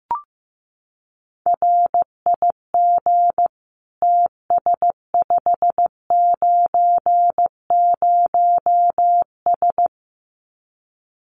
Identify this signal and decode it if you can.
Morse code, heard as a keyed tone, about 15 wpm, 700 Hz. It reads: RIG TS590S
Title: Common QSO Phrases Encore 15wpm